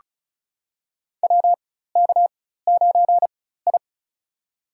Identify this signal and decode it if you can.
Morse code, heard as a keyed tone, about 34 wpm, spaced as for 16 wpm, 690 Hz. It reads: WK9I
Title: USA Call Signs - 4x Character Spacing - 35wpm